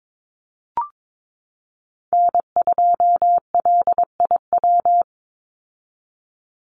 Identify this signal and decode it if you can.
Morse code, heard as a keyed tone, about 22 wpm, 700 Hz. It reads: N2LIW
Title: USA Call Signs 22wpm